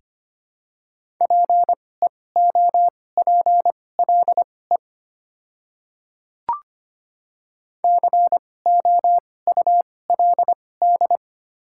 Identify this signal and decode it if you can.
Morse code, heard as a keyed tone, about 25 wpm, spaced as for 18 wpm, 700 Hz. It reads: PEOPLE COULD
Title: 2x Character Spacing - Top 100 Words - 25wpm